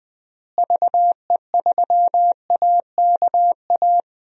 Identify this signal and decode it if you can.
Morse code, heard as a keyed tone, about 21 wpm, 700 Hz. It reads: VE3AKA
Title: DXCC-100 Call Signs - 6 Characters - Repeated 3x - 20wpm